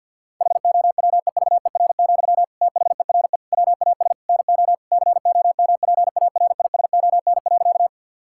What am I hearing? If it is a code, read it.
Morse code, heard as a keyed tone, about 50 wpm, 700 Hz. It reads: HOWEVER, THERE WAS NO COMPARISON.
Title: Sentences from Top 1600 Words 50wpm